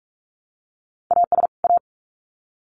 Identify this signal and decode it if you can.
Morse code, heard as a keyed tone, about 46 wpm, 700 Hz. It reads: AS A
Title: Sets of 2 Words 45wpm